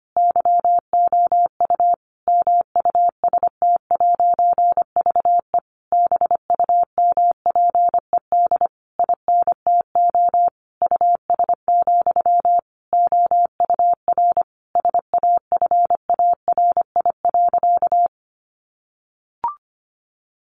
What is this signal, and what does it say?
Morse code, heard as a keyed tone, about 25 wpm, 700 Hz. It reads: YOU MUST'VE BUMPED INTO US, OUR SAFARI.
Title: A Sound Of Thunder 25wpm